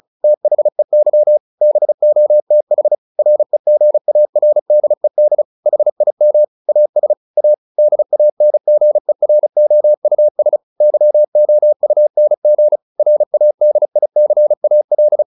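Morse code, heard as a keyed tone, about 36 wpm, 600 Hz. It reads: THEY BOTH REGARDED HIM AS A DANGEROUS YOUNG RADICAL
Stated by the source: Morse practice at 35 wpm from CW med Gnister, sending Ken Follett's novel Never in English.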